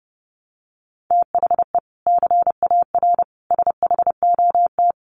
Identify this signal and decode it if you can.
Morse code, heard as a keyed tone, about 30 wpm, 700 Hz. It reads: THE CAR SHOT
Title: Sentences from Top 1300 Words 30wpm